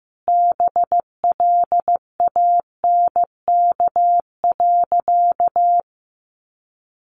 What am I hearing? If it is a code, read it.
Morse code, heard as a keyed tone, about 15 wpm, 700 Hz. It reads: BLANK.
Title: Sentences from Top 1500 Words 15wpm